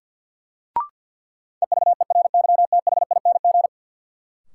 Morse code, heard as a keyed tone, about 52 wpm, 700 Hz. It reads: EVERYTHING
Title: Sentences from Top 700 Words - New Word Drill - 50wpm